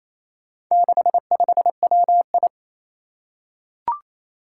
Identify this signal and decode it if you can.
Morse code, heard as a keyed tone, about 28 wpm, 700 Hz. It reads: 65WI
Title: Four Challenging Letter-Number Combinations 28wpm